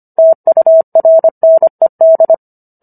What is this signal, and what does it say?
Morse code, heard as a keyed tone, about 25 wpm, 650 Hz. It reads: TURNED